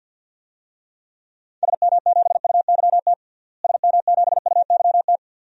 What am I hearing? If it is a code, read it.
Morse code, heard as a keyed tone, about 46 wpm, 700 Hz. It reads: SM7UYT SM7UYT
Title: DXCC-100 Call Signs - 6 Characters - Repeated 2x - 50wpm